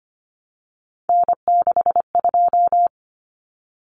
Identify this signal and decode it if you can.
Morse code, heard as a keyed tone, about 25 wpm, 700 Hz. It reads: N62
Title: Common Cut Numbers - 3-Digit 25wpm